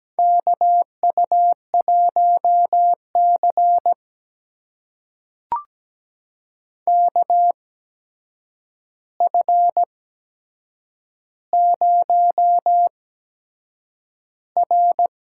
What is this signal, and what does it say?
Morse code, heard as a keyed tone, about 17 wpm, spaced as for 7 wpm, 700 Hz. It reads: KU1C KF0R
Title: USA Call Signs - 4 Characters - 8x Character Spacing - 17wpm